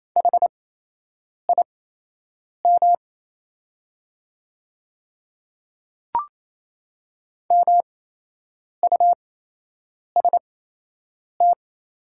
Morse code, heard as a keyed tone, about 28 wpm, spaced as for 8 wpm, 700 Hz. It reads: HIM MUST
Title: 8x Character Spacing - Top 200 Words - 28wpm